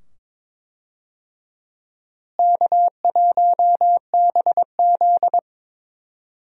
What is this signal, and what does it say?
Morse code, heard as a keyed tone, about 22 wpm, 700 Hz. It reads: K1BZ
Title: USA Call Signs - 4 Characters - Repeated 2x - 22wpm